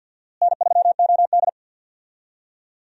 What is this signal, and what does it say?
Morse code, heard as a keyed tone, about 52 wpm, 700 Hz. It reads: N3OD
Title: USA Call Signs - 4 Characters - Repeated 2x - 50wpm